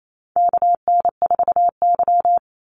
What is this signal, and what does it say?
Morse code, heard as a keyed tone, about 28 wpm, 700 Hz. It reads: KN4Y
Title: Call Signs Repeated 3x - 28wpm